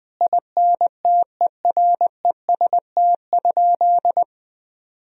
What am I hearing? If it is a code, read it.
Morse code, heard as a keyed tone, about 20 wpm, 700 Hz. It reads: INTEREST?
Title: Sentences from Top 1000 Words - Review Part 4 - 20wpm